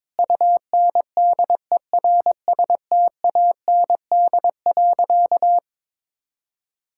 Morse code, heard as a keyed tone, about 21 wpm, 700 Hz. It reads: UNDERSTAND.
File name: practice / 22wpm / Sentences with 4-Word Phrases - 22wpm